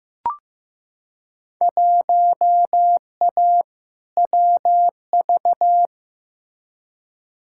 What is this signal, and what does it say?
Morse code, heard as a keyed tone, about 15 wpm, 700 Hz. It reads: 1A WV
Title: Field Day Exchanges Encore - 15wpm